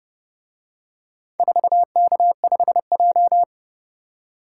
Morse code, heard as a keyed tone, about 31 wpm, 700 Hz. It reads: 4K5J